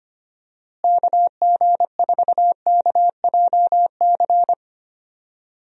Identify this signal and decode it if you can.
Morse code, heard as a keyed tone, about 25 wpm, 700 Hz. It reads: KG4KJC
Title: USA Call Signs - 6 Characters - Repeated 2x - 25wpm